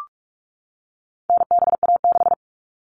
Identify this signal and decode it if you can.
Morse code, heard as a keyed tone, about 44 wpm, 700 Hz. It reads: NBA6
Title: All Cut Numbers - 4-Digit 45wpm